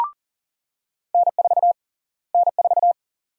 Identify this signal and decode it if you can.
Morse code, heard as a keyed tone, about 40 wpm, 700 Hz. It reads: N4 N4
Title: USA Call Sign Prefixes - Repeated 2x - 40wpm